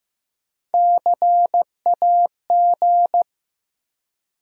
Morse code, heard as a keyed tone, about 15 wpm, 700 Hz. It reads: CAG
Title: USA Call Sign Postfixes - 3 Letters 15wpm